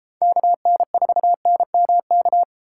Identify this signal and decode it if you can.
Morse code, heard as a keyed tone, about 31 wpm, 700 Hz. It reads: KN4NMK